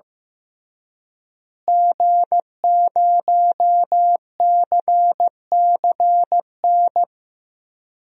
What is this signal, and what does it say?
Morse code, heard as a keyed tone, about 15 wpm, 700 Hz. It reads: G0CCN
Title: DXCC-100 Call Signs - 5 Characters - Repeated 3x - 15wpm